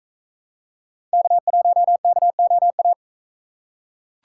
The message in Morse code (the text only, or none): K1KOA